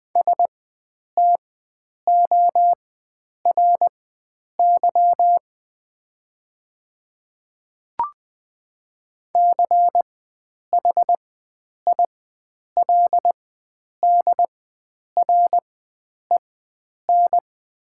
Morse code, heard as a keyed tone, about 20 wpm, spaced as for 9 wpm, 700 Hz. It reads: STORY CHILDREN
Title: 4x Character Spacing - Top 300 Words - 20wpm